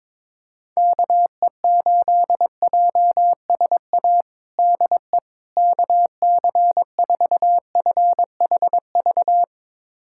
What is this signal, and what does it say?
Morse code, heard as a keyed tone, about 22 wpm, 700 Hz. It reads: KE8JSA DE KC4FHV
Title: USA Callsign DE Callsign - 6 Characters - 22wpm